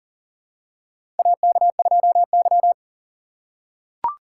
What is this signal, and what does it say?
Morse code, heard as a keyed tone, about 40 wpm, 700 Hz. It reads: AK2Y
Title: USA Call Signs - 4 Characters - 3x Character Spacing - 40wpm